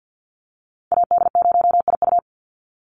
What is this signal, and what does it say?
Morse code, heard as a keyed tone, about 50 wpm, 700 Hz. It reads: AD0IU
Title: Call Signs - 50wpm